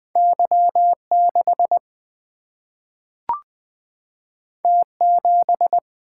Morse code, heard as a keyed tone, about 20 wpm, 700 Hz. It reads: Y6 T7
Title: Single-Letter Single-Number 20wpm